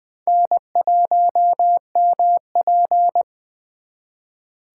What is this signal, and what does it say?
Morse code, heard as a keyed tone, about 25 wpm, 700 Hz. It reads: N1MP